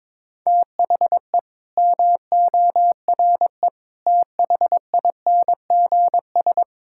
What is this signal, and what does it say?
Morse code, heard as a keyed tone, about 22 wpm, 700 Hz. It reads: THE MORE THINGS
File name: practice / 22wpm / Sets of 7 Words - 22wpm